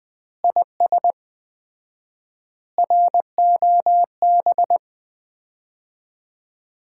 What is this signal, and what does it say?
Morse code, heard as a keyed tone, about 20 wpm, 700 Hz. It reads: IS ROB
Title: Common QSO Phrases - 4x Word-Spacing - 20wpm